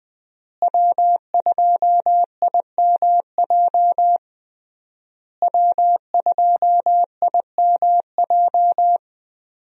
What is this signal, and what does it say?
Morse code, heard as a keyed tone, about 21 wpm, 700 Hz. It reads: W2IMJ W2IMJ
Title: USA Call Signs - 5 Characters - Repeated 3x - 20wpm